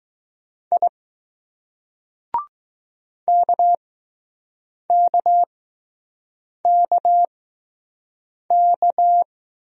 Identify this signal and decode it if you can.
Morse code, heard as a keyed tone, about 20 wpm, spaced as for 6 wpm, 700 Hz. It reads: I KKKK